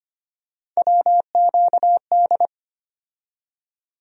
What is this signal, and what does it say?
Morse code, heard as a keyed tone, about 25 wpm, 700 Hz. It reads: WQD
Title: USA Call Sign Postfixes - 3 Letters 25wpm